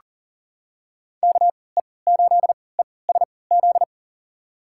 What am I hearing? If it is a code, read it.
Morse code, heard as a keyed tone, about 40 wpm, spaced as for 23 wpm, 700 Hz. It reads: KE8ESZ
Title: USA Call Signs - 6 Characters - 3x Character Spacing - 40wpm